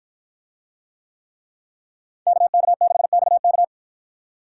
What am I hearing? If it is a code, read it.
Morse code, heard as a keyed tone, about 53 wpm, 700 Hz. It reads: KK6XK